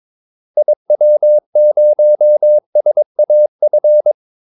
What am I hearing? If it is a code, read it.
Morse code, heard as a keyed tone, about 22 wpm, 600 Hz. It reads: IW0SAF
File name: Callsigns-22wpm